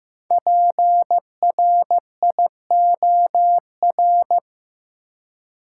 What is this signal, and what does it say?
Morse code, heard as a keyed tone, about 15 wpm, 700 Hz. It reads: PRIOR
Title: Five Letter Words - 15wpm